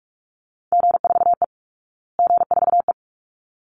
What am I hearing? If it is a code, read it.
Morse code, heard as a keyed tone, about 44 wpm, 700 Hz. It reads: G4E G4E
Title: Call Signs Repeated 3x - 45wpm